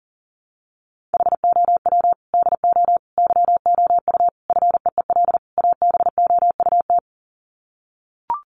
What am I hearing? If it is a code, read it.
Morse code, heard as a keyed tone, about 40 wpm, 700 Hz. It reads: HOW DO YOU FEEL ABOUT
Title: Sets of 5 Words 40wpm